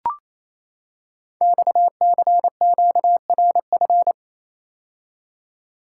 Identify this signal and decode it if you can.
Morse code, heard as a keyed tone, about 28 wpm, 700 Hz. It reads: XCQRF
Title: Code Groups - 5 Letters - 28wpm